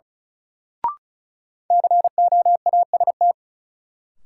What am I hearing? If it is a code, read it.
Morse code, heard as a keyed tone, about 34 wpm, 700 Hz. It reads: COAST
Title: Sentences from Top 1300 Words - New Word Drill Alt - 35wpm